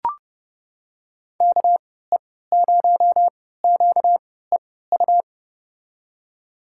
Morse code, heard as a keyed tone, about 32 wpm, spaced as for 17 wpm, 700 Hz. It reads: KE0QEU
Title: USA Call Signs - 6 Characters - 3x Character Spacing - 30wpm